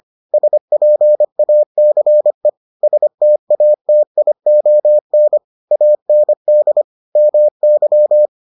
Morse code, heard as a keyed tone, about 25 wpm, 600 Hz. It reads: SPACE STATION AND MY